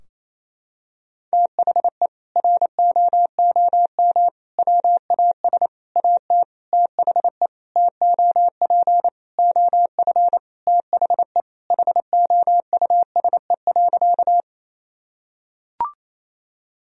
Morse code, heard as a keyed tone, about 28 wpm, 700 Hz. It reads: THE ROOM WAS AT THE TOP OF THE HOUSE.
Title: Sentences from Top 1000 Words - Review Part 2 - 28wpm